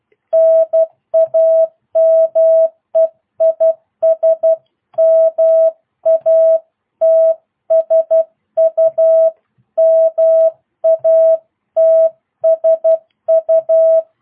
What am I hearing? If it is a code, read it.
Morse code, heard as a keyed tone, about 12 wpm, 650 Hz. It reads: NAME IS MATSU MATSU